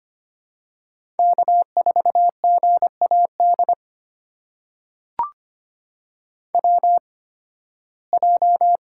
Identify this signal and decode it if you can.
Morse code, heard as a keyed tone, about 25 wpm, spaced as for 16 wpm, 700 Hz. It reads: K4GAD WJ
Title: USA Call Signs - 5 Characters - 8x Character Spacing - 25wpm